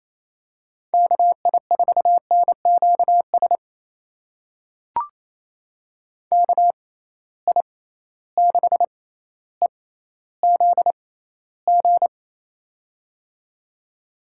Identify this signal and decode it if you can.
Morse code, heard as a keyed tone, about 28 wpm, spaced as for 14 wpm, 700 Hz. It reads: KI4NQS KI6EZG
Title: USA Call Signs - 6 Characters - 6x Character Spacing - 28wpm